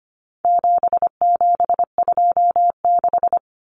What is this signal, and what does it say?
Morse code, heard as a keyed tone, about 25 wpm, 700 Hz. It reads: 7726
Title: Numbers - 4-Digit 25wpm